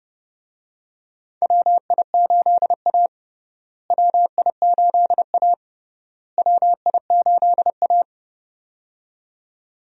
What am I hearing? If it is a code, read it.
Morse code, heard as a keyed tone, about 30 wpm, 700 Hz. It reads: WI8A WI8A WI8A
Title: USA Call Signs - 4 Characters - Repeated 3x - 30wpm